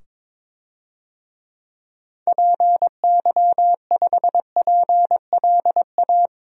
Morse code, heard as a keyed tone, about 22 wpm, 700 Hz. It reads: PY5PLA
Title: DXCC-100 Call Signs - 6 Characters - 22wpm